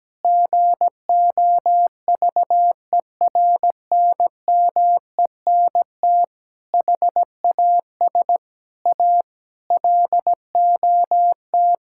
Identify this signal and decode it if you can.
Morse code, heard as a keyed tone, about 17 wpm, 700 Hz. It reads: GOVERNMENT HAS A LOT
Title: Sentences from Top 1000 Words - Review Part 5 - 17wpm